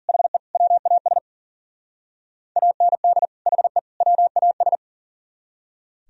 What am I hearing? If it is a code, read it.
Morse code, heard as a keyed tone, about 50 wpm, 700 Hz. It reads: HE WAS AND HE WAS